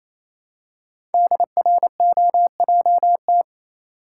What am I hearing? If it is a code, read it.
Morse code, heard as a keyed tone, about 29 wpm, 700 Hz. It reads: DROJT
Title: Code Groups - 5 Letters - 28wpm